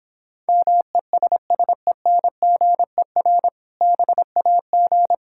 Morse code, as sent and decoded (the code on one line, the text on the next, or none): -- . ... ... -. --. .-. / -... .- --.
MESSENGER BAG